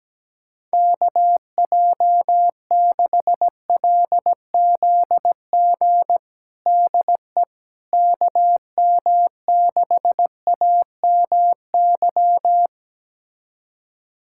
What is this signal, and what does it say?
Morse code, heard as a keyed tone, about 17 wpm, 700 Hz. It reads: KJ6LZG DE KM6AMY
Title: USA Callsign DE Callsign - 6 Characters - 17wpm